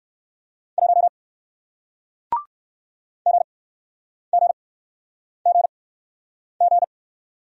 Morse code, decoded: 1 GGGG

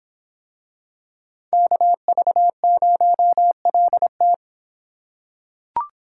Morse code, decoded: KV0LT

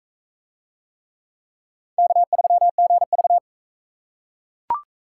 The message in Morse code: K3GV

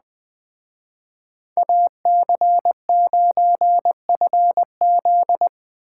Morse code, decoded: AC9FZ

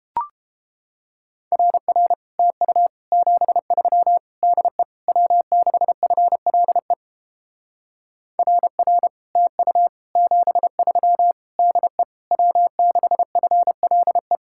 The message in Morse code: RR TU 73 DE W6FLE RR TU 73 DE W6FLE

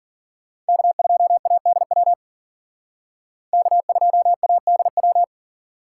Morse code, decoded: K2ADW K2ADW

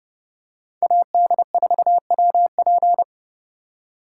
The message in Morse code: AD4WP

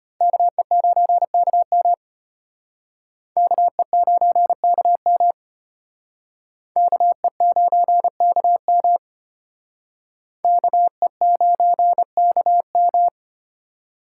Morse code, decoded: KE9KM KE9KM KE9KM KE9KM

